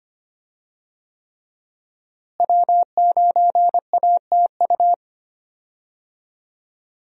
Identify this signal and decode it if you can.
Morse code, heard as a keyed tone, about 25 wpm, 700 Hz. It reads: W9ATU
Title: USA Call Signs - 5 Characters - Repeated 3x - 25wpm